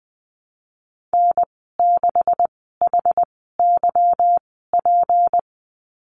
Morse code, heard as a keyed tone, about 19 wpm, spaced as for 14 wpm, 700 Hz. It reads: N6HYP
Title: 161 Lesson 26g - Letter g - Call Signs 20wpm